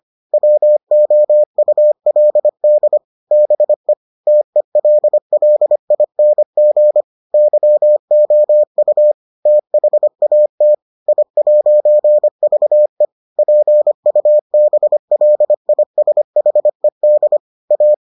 Morse code, heard as a keyed tone, about 25 wpm, 600 Hz. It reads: WOULD BE TELLING YOU THAT I'VE PUBLISHED A